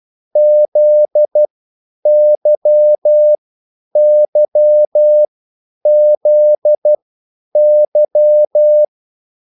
Morse code, heard as a keyed tone, about 12 wpm, spaced as for 9 wpm, 600 Hz. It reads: ZYYZY